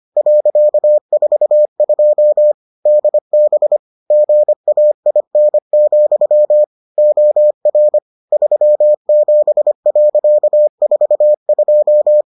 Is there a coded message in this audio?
Yes: Morse code from 0.2 to 12.2 s, heard as a keyed tone, about 25 wpm, 600 Hz. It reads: .42 DB GAIN, OR 37.42